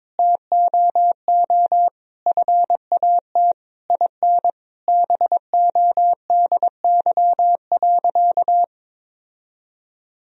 Morse code, heard as a keyed tone, about 22 wpm, 700 Hz. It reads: TOO FAT IN BODY.
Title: Sentences from Top 1000 Words - Review Part 4 - 22wpm